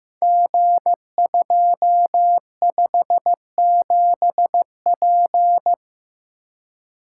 Morse code, heard as a keyed tone, about 15 wpm, 700 Hz. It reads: G257P
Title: Code Groups - 5 Letter:Numbers - 15wpm